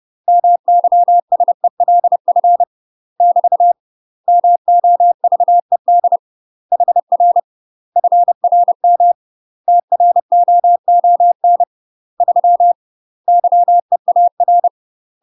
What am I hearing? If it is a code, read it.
Morse code, heard as a keyed tone, about 30 wpm, 700 Hz. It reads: MYSELF <BT> MOVED HR FRM TROON 3 YEAR
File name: UK MASTER QSOs TRAINING / 30WPM / UK MASTER QSOs 0043